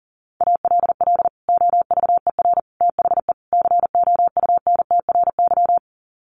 Morse code, heard as a keyed tone, about 40 wpm, 700 Hz. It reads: ALL OVER THE COUNTRY
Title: Sets of 4 Words 40wpm